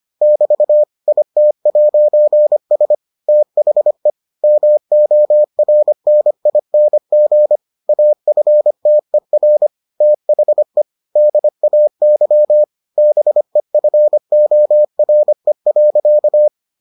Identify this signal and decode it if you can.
Morse code, heard as a keyed tone, about 25 wpm, 600 Hz. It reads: <BT> IT'S THE MORNING AFTER THE DAY BEFORE.